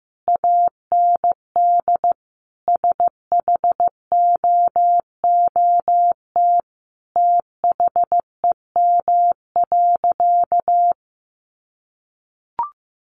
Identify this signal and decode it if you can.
Morse code, heard as a keyed tone, about 15 wpm, 700 Hz. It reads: AND SHOOT THEM.
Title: Sentences from Top 1200 Words 15wpm